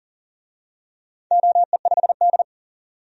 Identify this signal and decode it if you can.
Morse code, heard as a keyed tone, about 38 wpm, 700 Hz. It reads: OE5D
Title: DXCC-100 Call Signs - 4 Characters - 40wpm